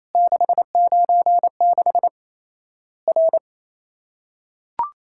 Morse code, heard as a keyed tone, about 28 wpm, 700 Hz. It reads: 696 R
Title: Worked All Europe DX Contest - QTC Exchange - 28wpm